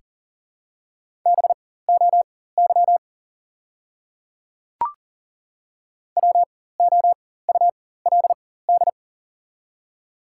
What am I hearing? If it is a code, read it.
Morse code, heard as a keyed tone, about 40 wpm, spaced as for 19 wpm, 700 Hz. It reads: BOY WOULD
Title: 4x Character Spacing - Top 300 Words - 40wpm